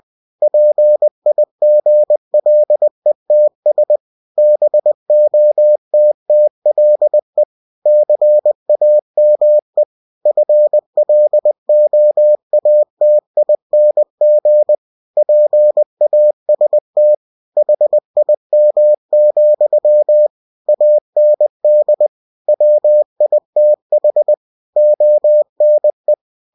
Morse code, heard as a keyed tone, about 20 wpm, 600 Hz. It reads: PIGLETS BOTTLE CAME FLOATING PAST HIM, AND WITH ONE
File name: cw-books / winnie-the-pooh / winnie-20wpm-page-0094